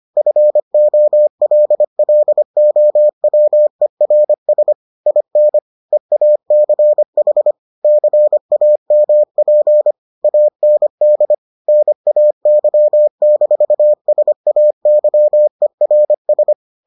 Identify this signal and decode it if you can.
Morse code, heard as a keyed tone, about 25 wpm, 600 Hz. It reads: FOLLOWERS IN EACH CAMP AND NAY-SAYERS